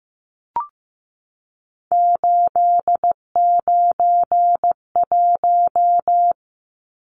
Morse code, heard as a keyed tone, about 15 wpm, 700 Hz. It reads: 891
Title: Numbers - 3-Digit 15wpm